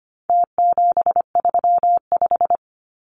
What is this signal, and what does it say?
Morse code, heard as a keyed tone, about 25 wpm, 700 Hz. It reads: T735
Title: Numbers - Contest Style - 4-Digit 25wpm